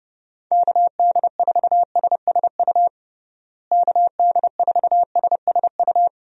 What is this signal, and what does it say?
Morse code, heard as a keyed tone, about 30 wpm, 700 Hz. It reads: KD4SSU KD4SSU
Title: USA Call Signs - 6 Characters - Repeated 3x - 30wpm